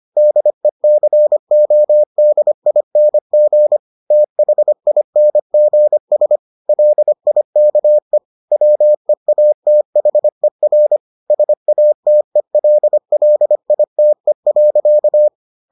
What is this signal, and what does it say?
Morse code, heard as a keyed tone, about 25 wpm, 600 Hz. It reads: DECODING THINGS LIKE WEATHER SATELLITE.